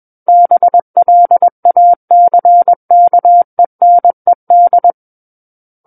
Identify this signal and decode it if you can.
Morse code, heard as a keyed tone, about 21 wpm, 700 Hz. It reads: BLACKENED
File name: Book One / 12-Chapter 12 - 21 WPM